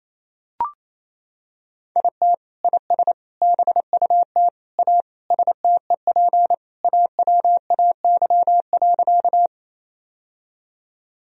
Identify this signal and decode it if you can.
Morse code, heard as a keyed tone, about 28 wpm, 700 Hz. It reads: IT IS BUT A STEP AWAY.
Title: Sentences from Top 1000 Words - Review Part 2 - 28wpm